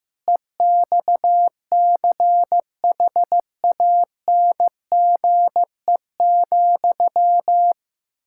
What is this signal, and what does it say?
Morse code, heard as a keyed tone, about 15 wpm, 700 Hz. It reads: EXCHANGE,
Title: Sentences from Top 1000 Words - Review Part 6 - 15wpm